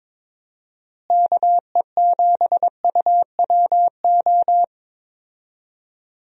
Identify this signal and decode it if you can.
Morse code, heard as a keyed tone, about 21 wpm, 700 Hz. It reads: KE7UWO
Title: USA Call Signs - 6 Characters - Repeated 3x - 22wpm